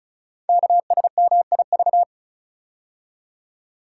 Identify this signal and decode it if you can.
Morse code, heard as a keyed tone, about 36 wpm, 690 Hz. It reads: KSMIV
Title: Code Groups - 5 Letters - 35wpm